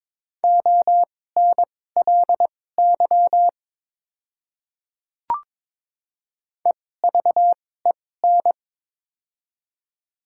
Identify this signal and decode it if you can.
Morse code, heard as a keyed tone, about 22 wpm, spaced as for 16 wpm, 700 Hz. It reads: ONLY EVEN
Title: 2x Character Spacing - Top 200 Words - 22wpm